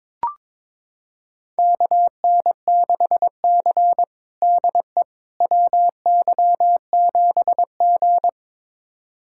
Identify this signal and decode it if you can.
Morse code, heard as a keyed tone, about 22 wpm, 700 Hz. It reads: KN6C DE WY7G
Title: USA Callsign DE Callsign - 4 Characters - 22wpm